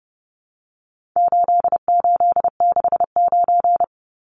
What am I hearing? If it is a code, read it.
Morse code, heard as a keyed tone, about 30 wpm, 700 Hz. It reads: 8869
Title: Numbers - 4-Digit 30wpm